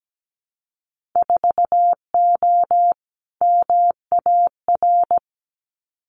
Morse code, heard as a keyed tone, about 17 wpm, 700 Hz. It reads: 4O MAR
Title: Winter Field Day Exchanges Encore 17wpm